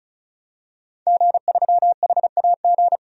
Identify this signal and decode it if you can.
Morse code, heard as a keyed tone, about 34 wpm, 700 Hz. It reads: G3HAG
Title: DXCC-100 Call Signs - 5 Characters - 35wpm